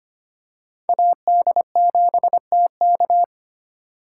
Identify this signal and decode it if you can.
Morse code, heard as a keyed tone, about 25 wpm, 700 Hz. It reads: AD7TK
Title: USA Call Signs - 2x Character Spacing - 25wpm